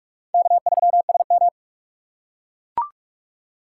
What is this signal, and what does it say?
Morse code, heard as a keyed tone, about 41 wpm, 700 Hz. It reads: K3SM
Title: USA Call Signs - 4 Characters - 8x Character Spacing - 45wpm